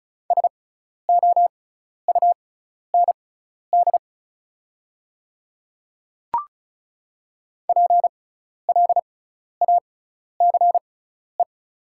Morse code, heard as a keyed tone, about 36 wpm, spaced as for 12 wpm, 690 Hz. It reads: SOUND PLACE
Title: 6x Character Spacing - Top 200 Words - 35wpm